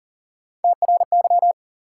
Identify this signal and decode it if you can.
Morse code, heard as a keyed tone, about 40 wpm, 700 Hz. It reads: TRY
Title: Sentences from Top 200 Words - New Word Drill - 40wpm